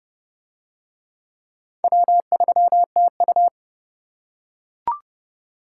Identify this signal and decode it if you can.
Morse code, heard as a keyed tone, about 29 wpm, 700 Hz. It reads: W3TU